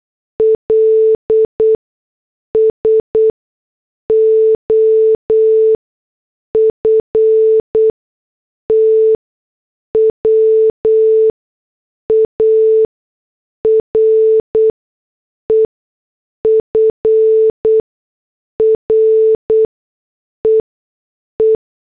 Morse code, heard as a keyed tone, about 8 wpm, spaced as for 6 wpm, 440 Hz. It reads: L SOFTWARE FREE